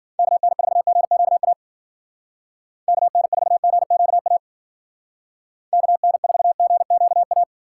Morse code, heard as a keyed tone, about 50 wpm, 700 Hz. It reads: KN4GQA KN4GQA KN4GQA
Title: Speed Racing - USA 2x3 Call Signs 40wpm